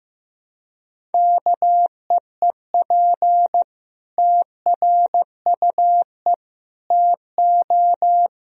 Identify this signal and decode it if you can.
Morse code, heard as a keyed tone, about 15 wpm, 700 Hz. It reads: KEEP TRUE TO
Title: Sentences from Top 1000 Words - Review Part 3 - 15wpm